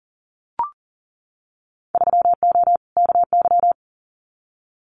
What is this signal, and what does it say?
Morse code, heard as a keyed tone, about 38 wpm, 700 Hz. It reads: 3O KY
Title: Winter Field Day Exchanges 40wpm